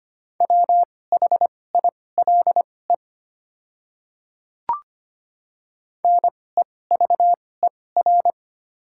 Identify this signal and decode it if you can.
Morse code, heard as a keyed tone, about 25 wpm, spaced as for 18 wpm, 690 Hz. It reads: WHILE NEVER
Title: 2x Character Spacing - Top 300 Words - 25wpm